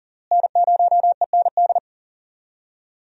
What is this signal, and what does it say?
Morse code, heard as a keyed tone, about 40 wpm, 700 Hz. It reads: N0END
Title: USA Call Signs - 5 Characters - Repeated 3x - 40wpm